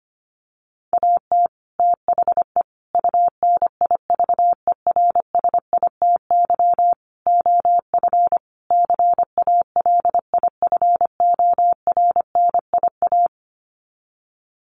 Morse code, heard as a keyed tone, about 25 wpm, 700 Hz. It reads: AT THE UNIVERSITY OF CALIFORNIA
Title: Sets of 5 Words Encore2 25wpm